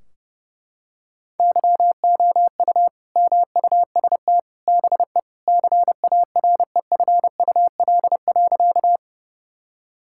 Morse code, heard as a keyed tone, about 30 wpm, 700 Hz. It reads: YOU MUST BE CAREFUL.
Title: Sentences from Top 1600 Words 30wpm